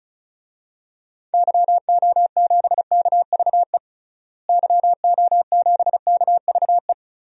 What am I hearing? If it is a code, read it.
Morse code, heard as a keyed tone, about 35 wpm, 700 Hz. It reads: YO7KVE YO7KVE
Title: DXCC-100 Call Signs - 6 Characters - Repeated 3x - 35wpm